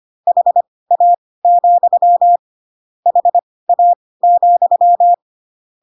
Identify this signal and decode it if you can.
Morse code, heard as a keyed tone, about 25 wpm, spaced as for 18 wpm, 700 Hz. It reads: HA, HA,
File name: Pinocchio / Pinocchio ch1 12 25 ik1hge